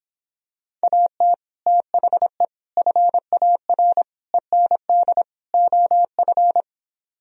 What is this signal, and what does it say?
Morse code, heard as a keyed tone, about 26 wpm, 700 Hz. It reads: AT THE FAR END OF